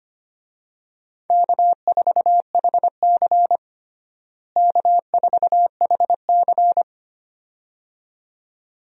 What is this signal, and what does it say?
Morse code, heard as a keyed tone, about 25 wpm, 700 Hz. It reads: K4HC K4HC
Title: USA Call Signs - 4 Characters - Repeated 2x - 25wpm